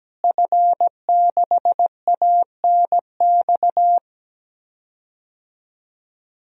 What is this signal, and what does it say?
Morse code, heard as a keyed tone, about 17 wpm, 700 Hz. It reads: F6ANX
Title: DXCC-100 Call Signs - 5 Characters - Repeated 2x - 17wpm